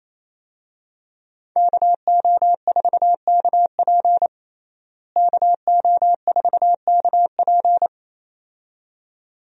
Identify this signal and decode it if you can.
Morse code, heard as a keyed tone, about 28 wpm, 700 Hz. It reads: KO4KP KO4KP
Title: USA Call Signs - 5 Characters - Repeated 2x - 28wpm